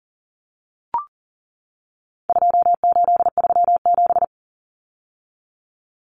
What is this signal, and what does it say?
Morse code, heard as a keyed tone, about 39 wpm, 700 Hz. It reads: 2837
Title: Common Cut Numbers - 4-Digit 40wpm